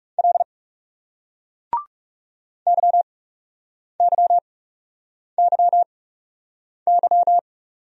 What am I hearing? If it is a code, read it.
Morse code, heard as a keyed tone, about 40 wpm, spaced as for 8 wpm, 700 Hz. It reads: L YYYY